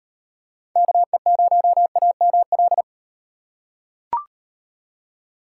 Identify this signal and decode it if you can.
Morse code, heard as a keyed tone, about 36 wpm, 700 Hz. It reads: KE0AML